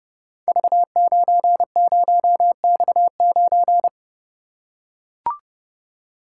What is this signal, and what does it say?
Morse code, heard as a keyed tone, about 30 wpm, 700 Hz. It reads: V90X9
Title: Code Groups - 5 Letter:Numbers - 30wpm